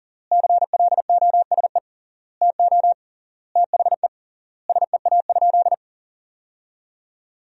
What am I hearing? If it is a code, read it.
Morse code, heard as a keyed tone, about 40 wpm, 700 Hz. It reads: CLOSE TO THE SEA?
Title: 3x Word-Spacing - Sentences from Top 300 Words - 40wpm